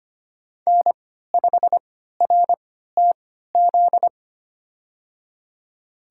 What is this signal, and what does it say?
Morse code, heard as a keyed tone, about 25 wpm, spaced as for 14 wpm, 700 Hz. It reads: N5RTZ
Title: USA Call Signs - 5 Characters - 3x Character Spacing - 25wpm